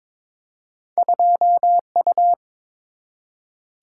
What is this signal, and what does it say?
Morse code, heard as a keyed tone, about 22 wpm, 700 Hz. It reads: 2U
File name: practice / 22wpm / Two Letter-Number Combinations - 22wpm